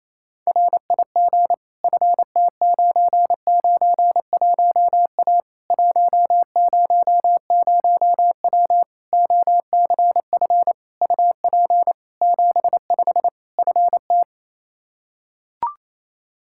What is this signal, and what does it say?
Morse code, heard as a keyed tone, about 27 wpm, 700 Hz. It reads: RIG FT991A 100W OCF UP 75 FT
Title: K1USN SST QSO Practice 28wpm